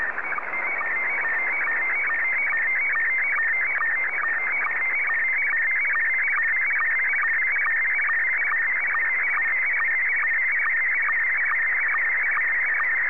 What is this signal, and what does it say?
SSTV (slow-scan television), Scottie 1.